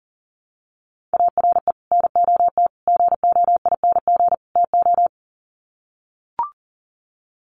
Morse code, heard as a keyed tone, about 40 wpm, 700 Hz. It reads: ARE NOT GOING TO
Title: Sets of 4 Words 40wpm